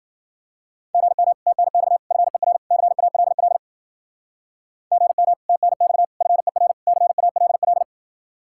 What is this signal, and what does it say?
Morse code, heard as a keyed tone, about 50 wpm, 700 Hz. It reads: GM TNX FER CALL GM TNX FER CALL
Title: Speed-Racing Common QSO Phrases Encore - 40wpm